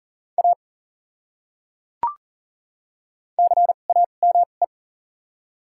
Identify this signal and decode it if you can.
Morse code, heard as a keyed tone, about 40 wpm, spaced as for 29 wpm, 700 Hz. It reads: A CAME
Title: 2x Character Spacing - Top 200 Words - 40wpm